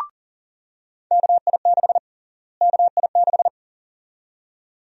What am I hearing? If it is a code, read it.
Morse code, heard as a keyed tone, about 37 wpm, 700 Hz. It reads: KI6 KI6
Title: USA Call Sign Prefixes - Repeated 2x - 40wpm